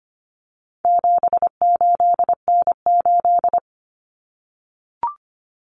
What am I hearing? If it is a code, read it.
Morse code, heard as a keyed tone, about 25 wpm, 700 Hz. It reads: 78N8
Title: Common Cut Numbers - 4-Digit 25wpm